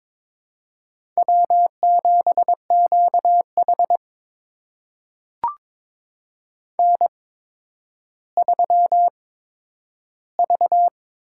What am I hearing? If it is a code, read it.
Morse code, heard as a keyed tone, about 22 wpm, spaced as for 11 wpm, 700 Hz. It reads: W7QH N3V
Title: USA Call Signs - 4 Characters - 8x Character Spacing - 22wpm